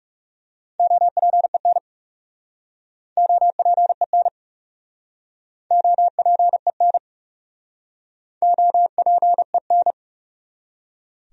Morse code, heard as a keyed tone, about 40 wpm, 700 Hz. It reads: OPEN OPEN OPEN OPEN